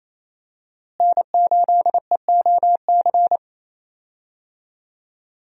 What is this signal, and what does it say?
Morse code, heard as a keyed tone, about 28 wpm, 700 Hz. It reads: N8EOC
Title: USA Call Signs - 5 Characters - Repeated 2x - 28wpm